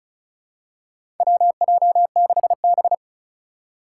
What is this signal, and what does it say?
Morse code, heard as a keyed tone, about 36 wpm, 700 Hz. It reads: WJ6B
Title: Four Challenging Letter-Number Combinations 35wpm